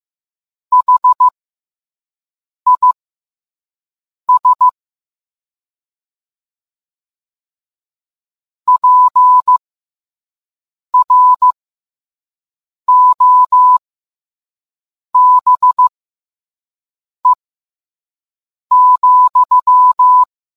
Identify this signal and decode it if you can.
Morse code, heard as a keyed tone, about 15 wpm, spaced as for 5 wpm, 1000 Hz. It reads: HIS PROBE,